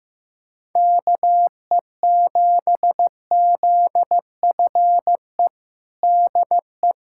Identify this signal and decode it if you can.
Morse code, heard as a keyed tone, about 15 wpm, 700 Hz. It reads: KE7ZFE DE
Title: USA Callsign DE Callsign 15wpm